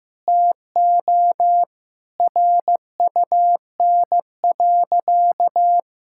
Morse code, heard as a keyed tone, about 16 wpm, 700 Hz. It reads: TO RUN.
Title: Sentences from Top 1600 Words 15wpm